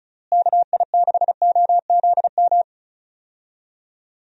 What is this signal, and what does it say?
Morse code, heard as a keyed tone, about 36 wpm, 700 Hz. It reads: KI6OZM
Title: USA Call Signs - 6 Characters - Repeated 3x - 35wpm